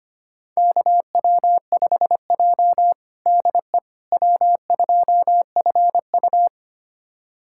Morse code, heard as a keyed tone, about 25 wpm, 700 Hz. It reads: KW5J DE W2FU
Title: USA Callsign DE Callsign - 4 Characters - 25wpm